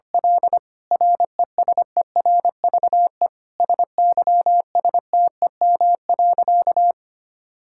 Morse code, heard as a keyed tone, about 25 wpm, 700 Hz. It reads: L RESERVE SYSTEM.
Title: Sentences from Top 1000 Words - Review Part 6 - 25wpm